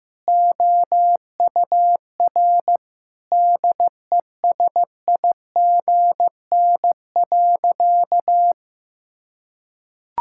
Morse code, heard as a keyed tone, about 15 wpm, 700 Hz. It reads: OUR DESIGN.
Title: Sentences from Top 1000 Words - Review Part 5 - 15wpm